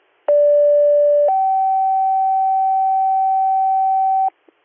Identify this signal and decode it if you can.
Two-tone page, heard as a one-second tone, then a three-second tone, about 580 then 770 Hz, with noise. It is heard from 0.3 to 4.3 s.